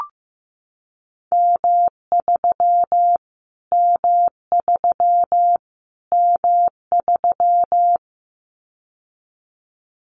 Morse code, heard as a keyed tone, about 15 wpm, 700 Hz. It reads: M3 M3 M3
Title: Callsign Prefixes for Top 25 DXCC - Repeated 3x - 15wpm